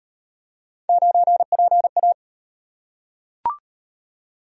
Morse code, heard as a keyed tone, about 37 wpm, 700 Hz. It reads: 9PA